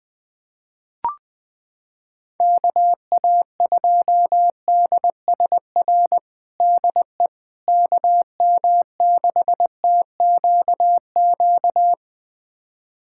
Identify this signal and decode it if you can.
Morse code, heard as a keyed tone, about 20 wpm, 700 Hz. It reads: KA2DSR DE KM6TQQ
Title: USA Callsign DE Callsign 20wpm